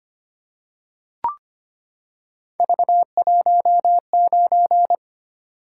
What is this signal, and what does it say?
Morse code, heard as a keyed tone, about 25 wpm, 700 Hz. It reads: V19
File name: Three Letter-Number Combinations - 25wpm